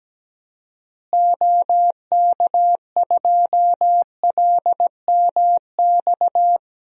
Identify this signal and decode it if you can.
Morse code, heard as a keyed tone, about 17 wpm, 700 Hz. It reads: OK2LMX
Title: DXCC-100 Call Signs - 6 Characters - Repeated 3x - 17wpm